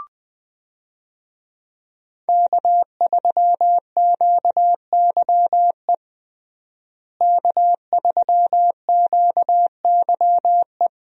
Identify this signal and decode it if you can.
Morse code, heard as a keyed tone, about 21 wpm, 700 Hz. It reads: K3QYE K3QYE
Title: USA Call Signs - 5 Characters - Repeated 3x - 20wpm